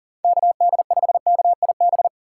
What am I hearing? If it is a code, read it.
Morse code, heard as a keyed tone, about 37 wpm, 700 Hz. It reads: KD5KIB
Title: USA Call Signs - 6 Characters - 3x Character Spacing - 40wpm